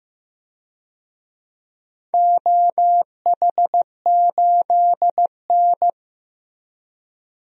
Morse code, heard as a keyed tone, about 15 wpm, 700 Hz. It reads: OH8N
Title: DXCC-100 Call Signs - 4 Characters - Repeated 3x - 15wpm